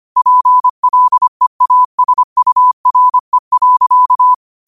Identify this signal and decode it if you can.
Morse code, heard as a keyed tone, about 25 wpm, 1000 Hz. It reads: PLEASURE.